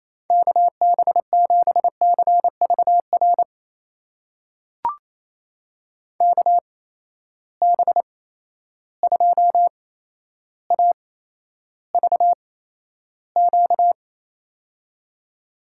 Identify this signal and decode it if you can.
Morse code, heard as a keyed tone, about 28 wpm, spaced as for 12 wpm, 700 Hz. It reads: KB7CVR KB2AVQ
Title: USA Call Signs - 6 Characters - 8x Character Spacing - 28wpm